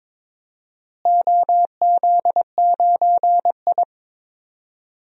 Morse code, heard as a keyed tone, about 22 wpm, 700 Hz. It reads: OZ9I
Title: DXCC-100 Call Signs - 4 Characters - Repeated 3x - 22wpm